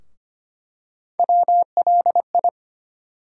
Morse code, heard as a keyed tone, about 25 wpm, 700 Hz. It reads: WLI
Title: USA Call Sign Postfixes - 3 Letters 25wpm